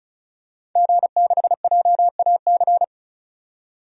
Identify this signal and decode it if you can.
Morse code, heard as a keyed tone, about 34 wpm, 700 Hz. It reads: G6JAC
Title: DXCC-100 Call Signs - 5 Characters - 35wpm